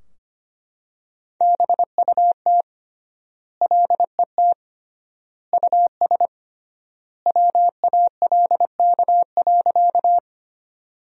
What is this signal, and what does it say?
Morse code, heard as a keyed tone, about 25 wpm, 700 Hz. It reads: BUT LET US WALK.
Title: 3x Word-Spacing - Sentences from Top 300 Words - 25wpm